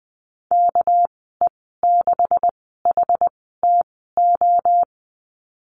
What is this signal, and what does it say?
Morse code, heard as a keyed tone, about 20 wpm, spaced as for 14 wpm, 700 Hz. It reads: KE6HTO
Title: 245 Lesson 39g - Letter z - Call Signs 20wpm